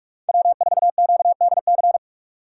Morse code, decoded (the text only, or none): W4QDC